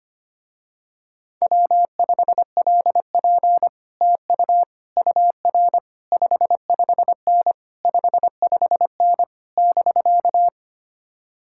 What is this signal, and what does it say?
Morse code, heard as a keyed tone, about 25 wpm, 700 Hz. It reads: W5LP TU UR 55N 55N <BK>
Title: Speed-Racing Common QSO Phrases - 25wpm